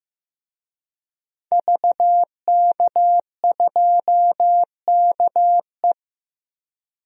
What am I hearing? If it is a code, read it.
Morse code, heard as a keyed tone, about 15 wpm, 700 Hz. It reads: VK2KE
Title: DXCC-100 Call Signs - 5 Characters - 15wpm